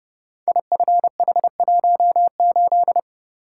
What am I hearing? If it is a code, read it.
Morse code, heard as a keyed tone, about 30 wpm, 700 Hz. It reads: IFH18
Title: Code Groups - 5 Letter:Numbers - 30wpm